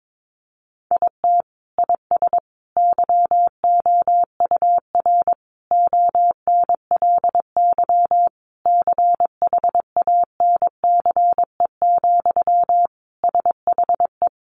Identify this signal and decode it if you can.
Morse code, heard as a keyed tone, about 22 wpm, 700 Hz. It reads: IT IS YOUR ONLY CHANCE, SHE
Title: Sentences from Top 800 Words 22wpm